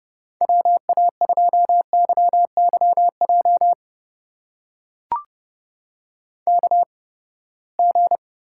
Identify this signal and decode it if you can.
Morse code, heard as a keyed tone, about 31 wpm, spaced as for 21 wpm, 700 Hz. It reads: WA2YYJ KG
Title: USA Call Signs - 6 Characters - 8x Character Spacing - 30wpm